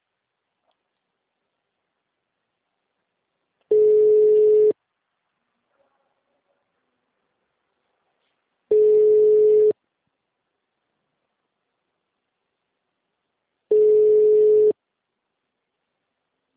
Ringback tone (Europe, 425 Hz).
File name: out- -2019- - - .22320